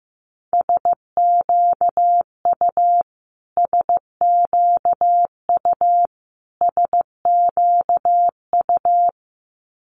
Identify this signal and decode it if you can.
Morse code, heard as a keyed tone, about 15 wpm, 700 Hz. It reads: SQU SQU SQU
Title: Multi-letter Phonemes Repeated 3x 15wpm